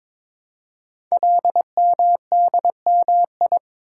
Morse code, heard as a keyed tone, about 22 wpm, 700 Hz. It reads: LMDMI
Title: Code Groups - 5 Letters - 22wpm